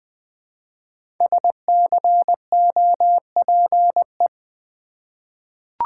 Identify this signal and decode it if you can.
Morse code, heard as a keyed tone, about 20 wpm, 700 Hz. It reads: SCOPE
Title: 5 Letter Words - First 3 Letters Given - 20wpm